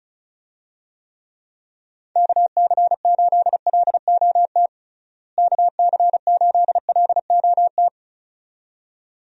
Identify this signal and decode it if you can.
Morse code, heard as a keyed tone, about 36 wpm, 700 Hz. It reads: KC8LOT KC8LOT
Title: USA Call Signs - 6 Characters - Repeated 2x - 35wpm